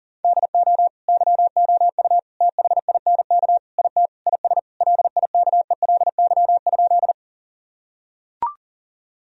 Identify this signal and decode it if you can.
Morse code, heard as a keyed tone, about 40 wpm, 700 Hz. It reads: DO YOU THINK IT IS LIKELY?
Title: Sentences from Top 1000 Words - Review Part 4 - 40wpm